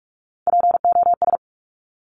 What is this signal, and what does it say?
Morse code, heard as a keyed tone, about 46 wpm, 700 Hz. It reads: POS
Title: Top Three Letter Combinations in Words 45wpm